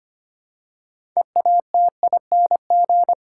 Morse code, heard as a keyed tone, about 25 wpm, 700 Hz. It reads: EATING
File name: practice / 25wpm / Difficult Sentences with Hints - 25wpm